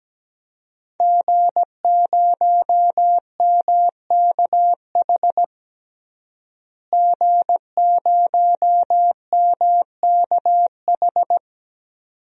Morse code, heard as a keyed tone, about 17 wpm, 700 Hz. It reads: G0MKH G0MKH
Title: DXCC-100 Call Signs - 5 Characters - Repeated 2x - 17wpm